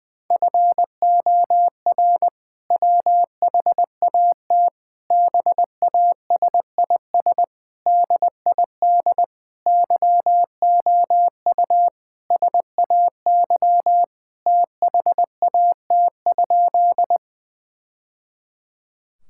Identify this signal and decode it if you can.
Morse code, heard as a keyed tone, about 20 wpm, 700 Hz. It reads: FOR WHAT BASIS DID YOU SAY THAT?
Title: Sentences from Top 1000 Words - Review Part 5 - 20wpm